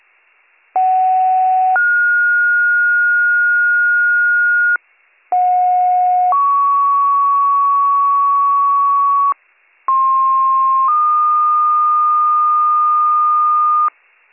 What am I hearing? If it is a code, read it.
Two-tone page, heard as a one-second tone, then a three-second tone, about 750 then 1430 Hz, about 730 then 1120 Hz, about 1060 then 1220 Hz. It is heard from 0.8 to 13.9 s.